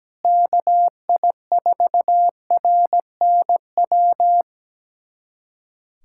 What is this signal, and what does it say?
Morse code, heard as a keyed tone, about 17 wpm, 700 Hz. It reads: KI4RNW
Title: USA Callsign DE Callsign - 6 Characters - 17wpm